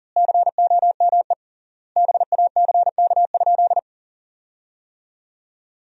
Morse code, heard as a keyed tone, about 40 wpm, 700 Hz. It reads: COME BACK?
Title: 3x Word-Spacing - Sentences from Top 300 Words - 40wpm